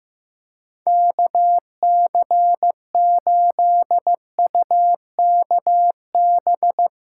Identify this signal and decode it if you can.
Morse code, heard as a keyed tone, about 15 wpm, 700 Hz. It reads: KC8UKB
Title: Speed Racing - USA 2x3 Call Signs 15wpm